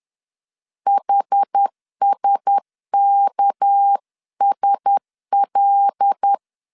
Morse code, heard as a keyed tone, about 10 wpm, 800 Hz. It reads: HSKSL
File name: LEZIONI / 7 GF9Z / 7 GF9Z 10wpm 2611